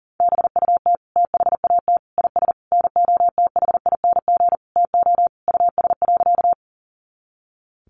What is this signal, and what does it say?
Morse code, heard as a keyed tone, about 40 wpm, 700 Hz. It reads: BUT THAT IS NOTHING TO US.
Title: Sentences from Top 500 Words 40wpm